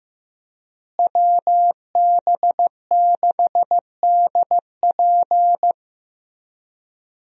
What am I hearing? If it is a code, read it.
Morse code, heard as a keyed tone, about 15 wpm, 700 Hz. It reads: WB6DP
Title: USA Call Signs - 5 Characters - Repeated 2x - 15wpm